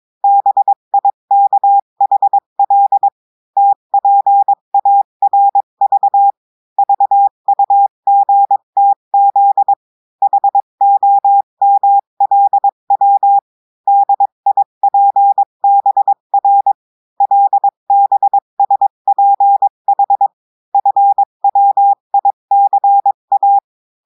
Morse code, heard as a keyed tone, about 22 wpm, 800 Hz. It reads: BIKHL TPARV VUGTZ HOMLW DIPBR LBSPH FWICA
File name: cours-du-mercredi / audios / cours-853-du-2-mai-2024-etxte chiffre et texte clair-a-20-wpm